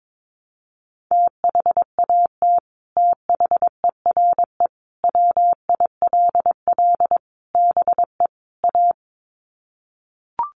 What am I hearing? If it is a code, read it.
Morse code, heard as a keyed tone, about 22 wpm, 700 Hz. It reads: THAT THERE WILL BE A
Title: Sets of 5 Words Encore2 22wpm